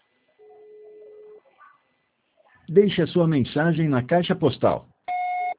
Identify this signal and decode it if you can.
Ringback tone (Europe, 425 Hz), with speech.
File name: out- -2011- - - .28353